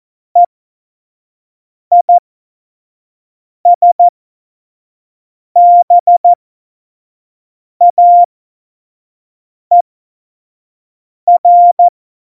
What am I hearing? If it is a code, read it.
Morse code, heard as a keyed tone, about 14 wpm, spaced as for 5 wpm, 700 Hz. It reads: EISBAER
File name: Morsekurs / Witze / Witz 20. November 2018, 25 F70